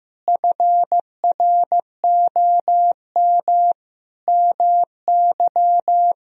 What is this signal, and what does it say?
Morse code, heard as a keyed tone, about 16 wpm, 700 Hz. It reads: FROM MY
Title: Sentences from Top 1000 Words - Review Part 1 - 15wpm